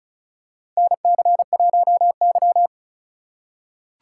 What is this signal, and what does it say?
Morse code, heard as a keyed tone, about 36 wpm, 700 Hz. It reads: NC1Y
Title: USA Call Signs - 4 Characters - 6x Character Spacing - 35wpm